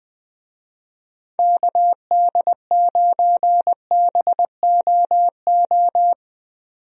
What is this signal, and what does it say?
Morse code, heard as a keyed tone, about 20 wpm, 700 Hz. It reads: KD9BOO
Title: USA Call Signs - 6 Characters - 20wpm